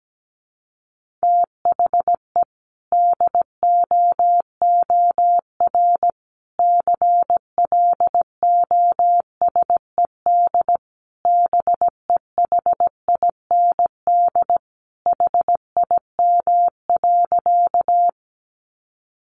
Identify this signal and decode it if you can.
Morse code, heard as a keyed tone, about 17 wpm, 700 Hz. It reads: THE DOOR CLOSED BEHIND HIM.
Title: Sentences from Top 1400 Words 17wpm